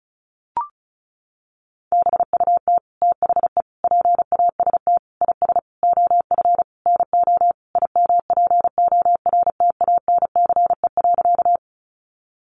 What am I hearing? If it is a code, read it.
Morse code, heard as a keyed tone, about 34 wpm, 700 Hz. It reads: BUT THE PAST IS OF NO IMPORTANCE.
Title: Sentences from Top 1200 Words 35wpm